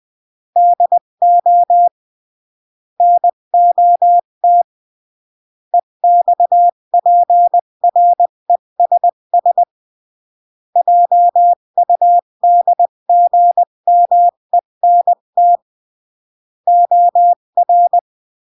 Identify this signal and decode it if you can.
Morse code, heard as a keyed tone, about 20 wpm, 700 Hz. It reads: DO NOT EXPRESS JUDGMENT OR
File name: CW / jodorowsky / 82 commandments 20-15